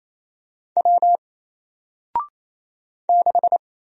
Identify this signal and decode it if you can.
Morse code, heard as a keyed tone, about 28 wpm, 700 Hz. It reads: W 6
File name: Single-Letter-Number-28wpm